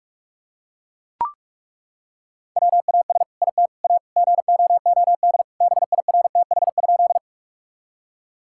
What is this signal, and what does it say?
Morse code, heard as a keyed tone, about 42 wpm, 700 Hz. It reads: WAS IT A GOOD BIRTH?
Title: Sentences from Top 1000 Words - Review Part 5 - 45wpm